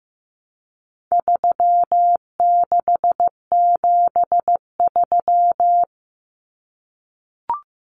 Morse code, heard as a keyed tone, about 15 wpm, 700 Hz. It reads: 3673
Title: Common Cut Numbers - 4-Digit 15wpm